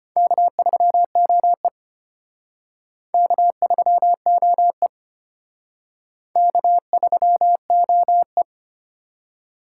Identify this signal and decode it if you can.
Morse code, heard as a keyed tone, about 31 wpm, 700 Hz. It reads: K3OE K3OE K3OE
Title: Speed Racing - USA Call Signs 25wpm